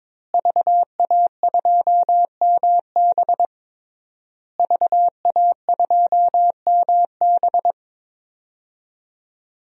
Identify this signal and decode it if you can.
Morse code, heard as a keyed tone, about 22 wpm, 700 Hz. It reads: VA2MB VA2MB
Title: DXCC-100 Call Signs - 5 Characters - Repeated 3x - 22wpm